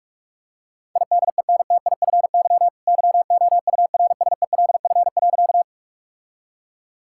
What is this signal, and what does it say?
Morse code, heard as a keyed tone, about 41 wpm, 700 Hz. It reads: IDENTIFY YOURSELF.
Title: Sentences from Top 1000 Words - Review Part 6 - 45wpm